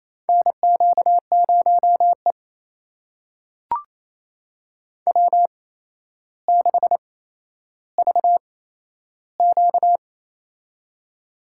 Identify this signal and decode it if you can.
Morse code, heard as a keyed tone, about 28 wpm, spaced as for 12 wpm, 700 Hz. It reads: NQ0E W6VQ
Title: USA Call Signs - 4 Characters - 8x Character Spacing - 28wpm